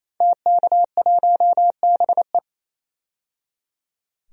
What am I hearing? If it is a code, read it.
Morse code, heard as a keyed tone, about 28 wpm, 700 Hz. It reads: TK1BE
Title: DXCC-100 Call Signs - 5 Characters - 28wpm